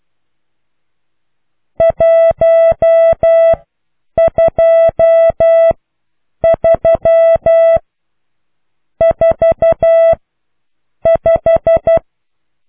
Morse code, heard as a keyed tone, about 12 wpm, spaced as for 7 wpm, 660 Hz. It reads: 12345